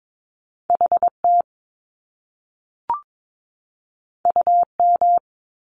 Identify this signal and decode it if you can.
Morse code, heard as a keyed tone, about 22 wpm, 700 Hz. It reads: HT UM
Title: Top Two Letter Combinations in Words 22wpm